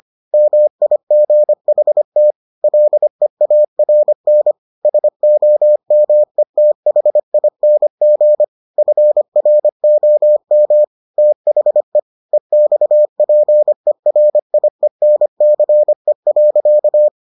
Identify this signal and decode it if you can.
Morse code, heard as a keyed tone, about 25 wpm, 600 Hz. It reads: MIGHT LEARN SOMETHING FROM THE EXPERIENCE.